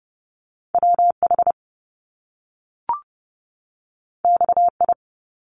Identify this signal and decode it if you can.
Morse code, heard as a keyed tone, about 30 wpm, 700 Hz. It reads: WH XI
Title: Top Two Letter Combinations in Words 30wpm